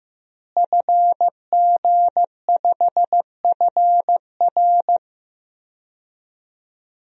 Morse code, heard as a keyed tone, about 15 wpm, 700 Hz. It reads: FG5FR
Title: DXCC-100 Call Signs - 5 Characters - Repeated 2x - 15wpm